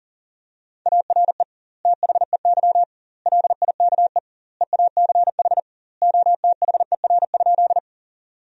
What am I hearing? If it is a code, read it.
Morse code, heard as a keyed tone, about 40 wpm, 700 Hz. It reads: ARE THEY LIKE EACH OTHER?
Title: 2x Word-Spacing - Sentences from Top 200 Words - 40wpm